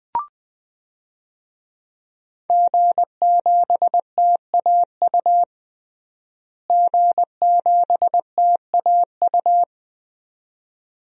Morse code, heard as a keyed tone, about 20 wpm, 700 Hz. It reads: G7TAU G7TAU
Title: DXCC-100 Call Signs - 5 Characters - Repeated 2x - 20wpm